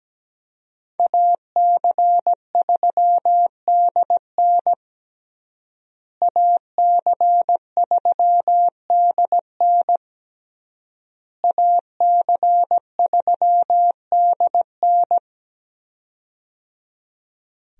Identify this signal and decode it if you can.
Morse code, heard as a keyed tone, about 17 wpm, 700 Hz. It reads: AC3DN AC3DN AC3DN
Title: USA Call Signs - 5 Characters - Repeated 3x - 17wpm